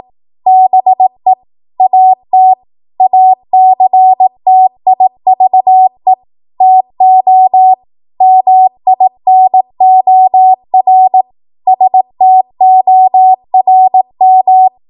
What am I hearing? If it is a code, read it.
Morse code, heard as a keyed tone, about 18 wpm, 750 Hz. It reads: BE AT ACTIVE TO MINOR STORM
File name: Morse / 18WPM